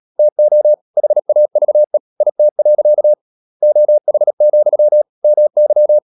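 Morse code, heard as a keyed tone, about 37 wpm, 600 Hz. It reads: TO HAVE IT. OH, MY
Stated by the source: Morse code practice at 35–39 wpm sending English text of Ken Follett's novel Never, from CW med Gnister.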